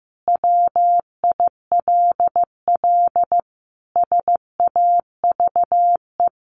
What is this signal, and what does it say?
Morse code, heard as a keyed tone, about 15 wpm, 700 Hz. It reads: WILL SAVE
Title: Sentences from Top 1100 Words 15wpm